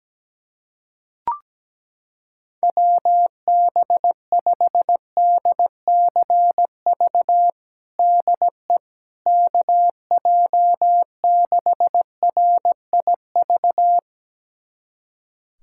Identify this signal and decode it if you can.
Morse code, heard as a keyed tone, about 17 wpm, 700 Hz. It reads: WB5DCV DE KJ6RIV
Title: USA Callsign DE Callsign - 6 Characters - 17wpm